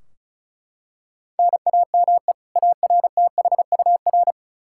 Morse code, heard as a keyed tone, about 36 wpm, 700 Hz. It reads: NAME ARTHUR
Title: Practice QSO - Exchange 1 - Beginning - 35wpm